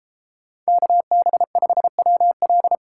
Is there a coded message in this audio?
Yes: Morse code from 0.7 to 2.8 s, heard as a keyed tone, about 32 wpm, 700 Hz. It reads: KB5WL